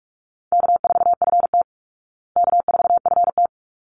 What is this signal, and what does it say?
Morse code, heard as a keyed tone, about 44 wpm, 700 Hz. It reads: K4FT K4FT
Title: Easy Call Signs Repeated 2x - 45wpm